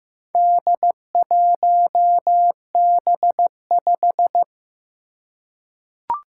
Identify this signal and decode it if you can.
Morse code, heard as a keyed tone, about 15 wpm, 700 Hz. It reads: D1B5
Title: Four Challenging Letter-Number Combinations 15wpm